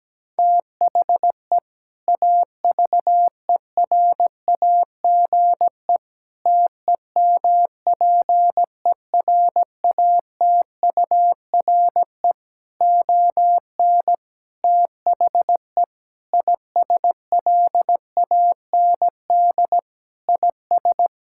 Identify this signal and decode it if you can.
Morse code, heard as a keyed tone, about 17 wpm, 700 Hz. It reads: THE AVERAGE TEMPERATURE ON THE ISLAND IS
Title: Sentences from Top 1000 Words - Review Part 6 - 17wpm